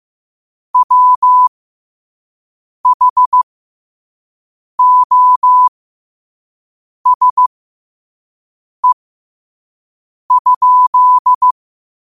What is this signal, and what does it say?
Morse code, heard as a keyed tone, about 15 wpm, spaced as for 5 wpm, 1000 Hz. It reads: WHOSE?
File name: qotdfiles / 5wpm